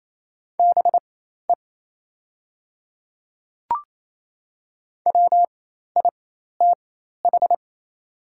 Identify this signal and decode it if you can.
Morse code, heard as a keyed tone, about 28 wpm, spaced as for 13 wpm, 690 Hz. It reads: BE WITH
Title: 4x Character Spacing - Top 200 Words - 28wpm